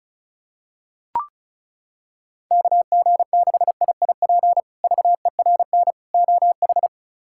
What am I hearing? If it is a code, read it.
Morse code, heard as a keyed tone, about 34 wpm, 700 Hz. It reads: KG6IIP VERN OH
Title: K1USN SST QSO Practice - 35wpm